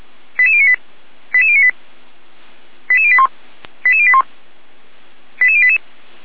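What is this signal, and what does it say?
Five-tone selective call (ZVEI), heard as quick runs of five tones, tones of about 80 ms. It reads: 88098 88098 88081 88081 88088
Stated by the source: ZVEI selective call 88571 received on an SDR.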